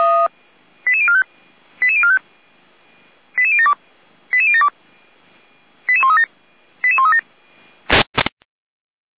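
Five-tone selective call (ZVEI), heard as quick runs of five tones, tones of about 70 ms. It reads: 88045 88045 88072 88072 88137 88137
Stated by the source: ZVEI five-tone selective call received on an SDR.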